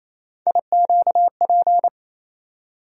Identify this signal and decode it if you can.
Morse code, heard as a keyed tone, about 28 wpm, 700 Hz. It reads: IQP
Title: USA Call Sign Postfixes - 3 Letters 28wpm